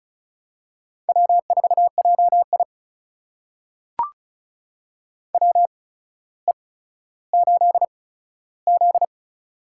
Morse code, heard as a keyed tone, about 36 wpm, spaced as for 15 wpm, 700 Hz. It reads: W4JI WE8Z
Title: USA Call Signs - 4 Characters - 8x Character Spacing - 35wpm